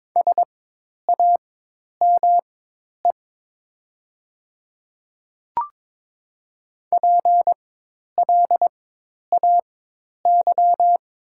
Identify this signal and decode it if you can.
Morse code, heard as a keyed tone, about 22 wpm, spaced as for 10 wpm, 700 Hz. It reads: SAME PLAY
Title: 4x Character Spacing - Top 200 Words - 22wpm